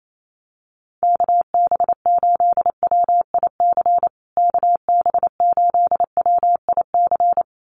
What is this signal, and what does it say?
Morse code, heard as a keyed tone, about 28 wpm, 700 Hz. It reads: KB8WIC KB8WIC
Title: USA Call Signs Repeated 2x 28wpm